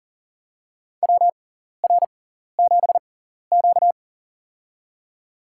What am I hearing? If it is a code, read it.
Morse code, heard as a keyed tone, about 40 wpm, spaced as for 14 wpm, 700 Hz. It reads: WR7Q
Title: USA Call Signs - 4 Characters - 6x Character Spacing - 40wpm